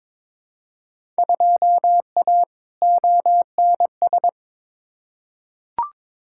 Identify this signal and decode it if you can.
Morse code, heard as a keyed tone, about 22 wpm, 700 Hz. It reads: 2A ONS
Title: Field Day Exchanges Encore - 22wpm